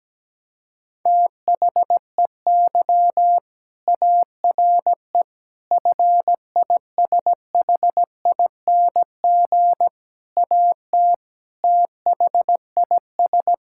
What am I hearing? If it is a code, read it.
Morse code, heard as a keyed tone, about 17 wpm, 700 Hz. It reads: THEY ARE FISHING AT THIS
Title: Sentences from Top 1000 Words - Review Part 5 - 17wpm